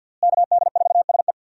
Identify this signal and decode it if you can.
Morse code, heard as a keyed tone, about 45 wpm, 700 Hz. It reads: KD4SE
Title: USA Call Signs - 5 Characters - Repeated 2x - 50wpm